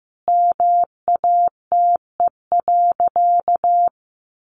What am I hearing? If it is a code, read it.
Morse code, heard as a keyed tone, about 15 wpm, 700 Hz. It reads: MATE.
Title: A Sound Of Thunder 15wpm